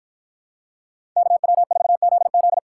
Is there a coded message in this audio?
Yes: Morse code from 1.2 to 2.6 s, heard as a keyed tone, about 53 wpm, 700 Hz. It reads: KW4ZZ